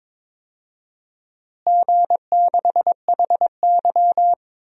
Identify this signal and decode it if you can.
Morse code, heard as a keyed tone, about 22 wpm, 700 Hz. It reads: G6HY
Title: DXCC-100 Call Signs - 4 Characters - Repeated 2x - 22wpm